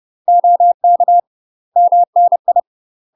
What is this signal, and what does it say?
Morse code, heard as a keyed tone, about 30 wpm, 700 Hz. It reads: OK MNI
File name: UK MASTER QSOs TRAINING / 30WPM / UK MASTER QSOs 0041